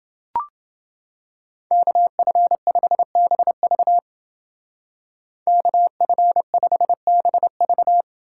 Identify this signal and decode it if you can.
Morse code, heard as a keyed tone, about 28 wpm, 700 Hz. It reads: KF5BV KF5BV